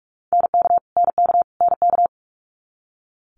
Morse code, heard as a keyed tone, about 46 wpm, 700 Hz. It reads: NK NK NK
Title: Multi-letter Phonemes Repeated 3x 45wpm